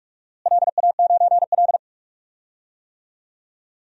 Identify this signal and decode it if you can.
Morse code, heard as a keyed tone, about 45 wpm, 700 Hz. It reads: LA9L
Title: DXCC-100 Call Signs - 4 Characters - Repeated 3x - 45wpm